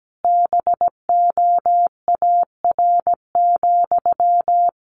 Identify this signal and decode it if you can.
Morse code, heard as a keyed tone, about 17 wpm, 700 Hz. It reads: BOAR,
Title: A Sound Of Thunder 17wpm